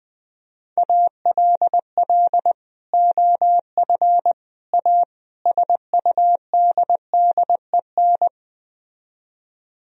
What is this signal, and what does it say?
Morse code, heard as a keyed tone, about 20 wpm, 700 Hz. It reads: ALL OF A SUDDEN
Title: Speed-Racing - Sets of 4 Words 17wpm